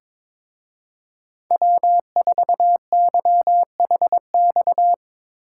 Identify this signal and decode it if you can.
Morse code, heard as a keyed tone, about 24 wpm, 700 Hz. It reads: W4YHX
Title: Speed Racing - USA Call Signs 22wpm